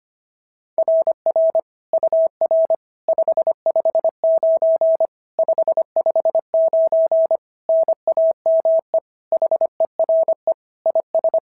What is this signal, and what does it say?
Morse code, heard as a keyed tone, about 24 wpm, 650 Hz. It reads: RR UR 559 559 NAME HERE IS
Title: SOTA QSO Practice 25wpm